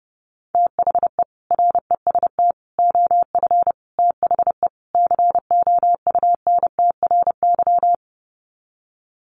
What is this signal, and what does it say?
Morse code, heard as a keyed tone, about 30 wpm, 700 Hz. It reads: THE REST OF THE COUNTRY
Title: Sets of 5 Words 30wpm